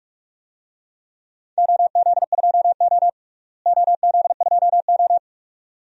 Morse code, heard as a keyed tone, about 44 wpm, 700 Hz. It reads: OZ2O OZ2O
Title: DXCC-100 Call Signs - 4 Characters - Repeated 2x - 45wpm